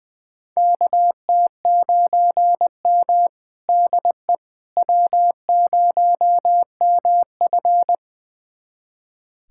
Morse code, heard as a keyed tone, about 21 wpm, 700 Hz. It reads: KT9M DE W0MF
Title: USA Callsign DE Callsign - 4 Characters - 20wpm